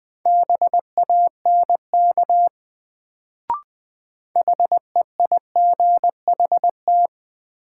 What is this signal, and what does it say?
Morse code, heard as a keyed tone, about 20 wpm, 700 Hz. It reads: BANK HEIGHT
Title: Sets of 5 Words - New Word Drill - 20wpm